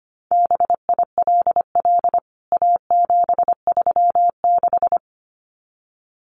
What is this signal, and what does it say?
Morse code, heard as a keyed tone, about 25 wpm, 700 Hz. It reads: BILL A736
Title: CWT Exchange 25wpm